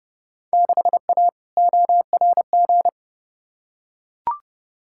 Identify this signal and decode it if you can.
Morse code, heard as a keyed tone, about 29 wpm, 700 Hz. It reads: 6A ORG
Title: Field Day Exchanges Encore - 30wpm